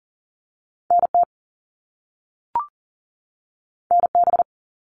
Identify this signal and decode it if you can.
Morse code, heard as a keyed tone, about 38 wpm, 700 Hz. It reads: NT NB
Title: Canadian Provinces and Territories 40wpm